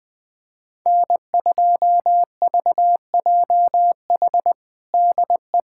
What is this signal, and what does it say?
Morse code, heard as a keyed tone, about 20 wpm, 700 Hz. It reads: N2VJH DE
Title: USA Callsign DE Callsign 20wpm